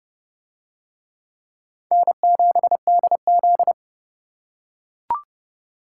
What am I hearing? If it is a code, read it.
Morse code, heard as a keyed tone, about 26 wpm, 700 Hz. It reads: N7DZ